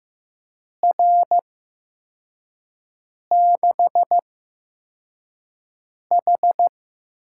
Morse code, heard as a keyed tone, about 15 wpm, spaced as for 4 wpm, 700 Hz. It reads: R6H
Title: USA Call Signs - 4 Characters - 8x Character Spacing - 15wpm